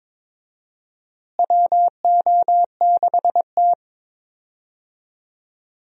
Morse code, heard as a keyed tone, about 22 wpm, 700 Hz. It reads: WO6T
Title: USA Call Signs - 4 Characters - Repeated 3x - 22wpm